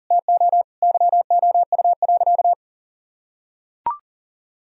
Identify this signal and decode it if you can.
Morse code, heard as a keyed tone, about 40 wpm, 700 Hz. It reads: TO YOU.
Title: Sentences from Top 1000 Words - Review Part 1 - 40wpm